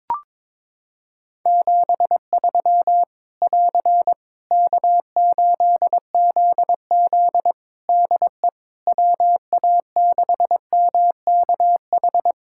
Morse code, heard as a keyed tone, about 22 wpm, 700 Hz. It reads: 73 <AR> K8ZZ DE WA6MKH
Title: K1USN SST QSO Practice 22wpm